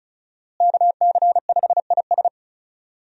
Morse code, heard as a keyed tone, about 36 wpm, 700 Hz. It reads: KC5IS
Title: USA Call Signs - 5 Characters - Repeated 3x - 35wpm